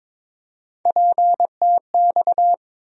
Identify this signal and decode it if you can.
Morse code, heard as a keyed tone, about 22 wpm, 700 Hz. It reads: PTX
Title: USA Call Sign Postfixes - 3 Letters 22wpm